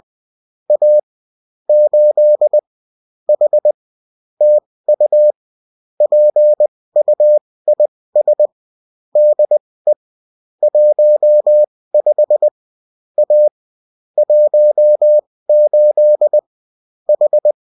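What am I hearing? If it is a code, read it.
Morse code, heard as a keyed tone, about 20 wpm, spaced as for 16 wpm, 600 Hz. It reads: A 8 H TU PUIS DE 15 A 18 H
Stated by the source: Bulletin in text and CW form.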